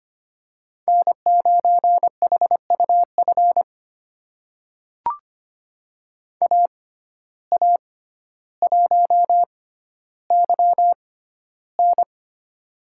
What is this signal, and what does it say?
Morse code, heard as a keyed tone, about 25 wpm, spaced as for 13 wpm, 700 Hz. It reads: N9HUF AA1YN
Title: USA Call Signs - 5 Characters - 6x Character Spacing - 25wpm